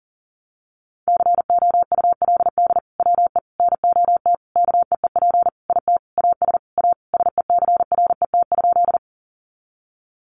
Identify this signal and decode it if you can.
Morse code, heard as a keyed tone, about 40 wpm, 700 Hz. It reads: COULD WE NOT KEEP IT AS A SECRET?
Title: Sentences from Top 1200 Words 40wpm